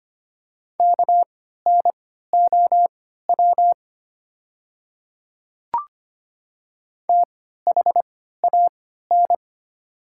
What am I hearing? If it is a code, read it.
Morse code, heard as a keyed tone, about 25 wpm, spaced as for 14 wpm, 700 Hz. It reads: KNOW THAN
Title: 3x Character Spacing - Top 100 Words - 25wpm